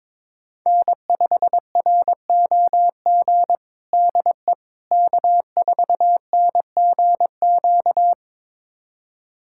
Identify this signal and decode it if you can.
Morse code, heard as a keyed tone, about 21 wpm, 700 Hz. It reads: N5ROG DE K4NGQ
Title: USA Callsign DE Callsign - 5 Characters - 22wpm